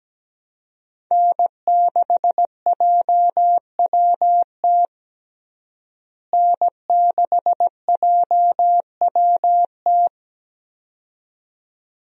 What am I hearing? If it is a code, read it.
Morse code, heard as a keyed tone, about 17 wpm, 700 Hz. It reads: N6JWT N6JWT
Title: USA Call Signs - 5 Characters - Repeated 2x - 17wpm